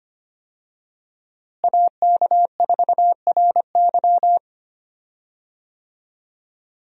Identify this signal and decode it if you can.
Morse code, heard as a keyed tone, about 25 wpm, 700 Hz. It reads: AK4RY
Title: USA Call Signs - 5 Characters - Repeated 3x - 25wpm